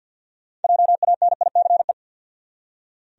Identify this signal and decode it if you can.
Morse code, heard as a keyed tone, about 48 wpm, 700 Hz. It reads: JANICE
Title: USA Common Names - 50wpm